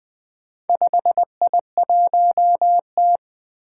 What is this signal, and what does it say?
Morse code, heard as a keyed tone, about 20 wpm, 700 Hz. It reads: 5I1T
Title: Four Challenging Letter-Number Combinations 20wpm